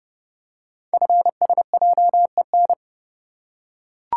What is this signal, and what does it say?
Morse code, heard as a keyed tone, about 29 wpm, 700 Hz. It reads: FSJEN
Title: Code Groups - 5 Letters - 30wpm